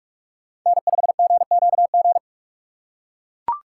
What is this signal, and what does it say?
Morse code, heard as a keyed tone, about 40 wpm, 700 Hz. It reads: N5GQG